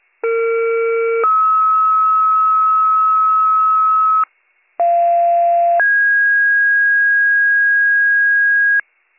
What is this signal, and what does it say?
Two-tone page, heard as a one-second tone, then a three-second tone, about 460 then 1290 Hz, about 690 then 1650 Hz, with noise. It is heard from 0.2 to 8.8 s.